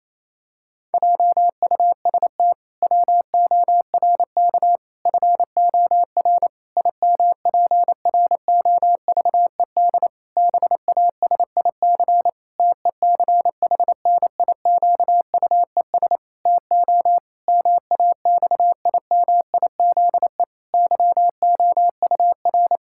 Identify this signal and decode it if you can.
Morse code, heard as a keyed tone, about 28 wpm, 700 Hz. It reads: JUST WORK FOR IMPROVED BASIC TECHNIQUES TO MAXIMIZE YOUR
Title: Pangram Challenge v2 - 28wpm